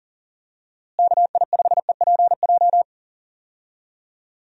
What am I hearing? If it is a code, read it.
Morse code, heard as a keyed tone, about 40 wpm, 700 Hz. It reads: KI5EPJ
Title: USA Call Signs - 6 Characters - Repeated 2x - 40wpm